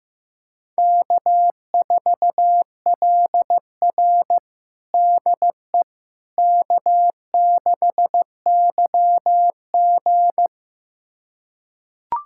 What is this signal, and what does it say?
Morse code, heard as a keyed tone, about 15 wpm, 700 Hz. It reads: K4LR DE K6YG
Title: USA Callsign DE Callsign - 4 Characters - 15wpm